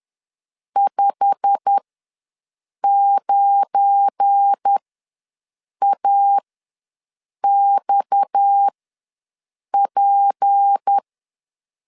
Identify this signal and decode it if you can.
Morse code, heard as a keyed tone, about 10 wpm, spaced as for 6 wpm, 800 Hz. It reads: 59AXP
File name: LEZIONI / 10 SIGNS / 10A SLASH ws 10wpm 4121